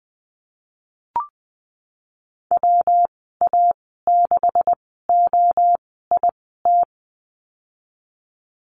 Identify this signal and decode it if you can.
Morse code, heard as a keyed tone, about 20 wpm, spaced as for 15 wpm, 700 Hz. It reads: WA6OIT
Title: 193 Lesson 31g - Letter v - Call Signs 20wpm